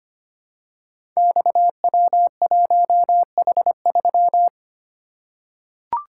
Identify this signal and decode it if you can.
Morse code, heard as a keyed tone, about 25 wpm, 700 Hz. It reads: XW1H3
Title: Code Groups - 5 Letter:Numbers - 25wpm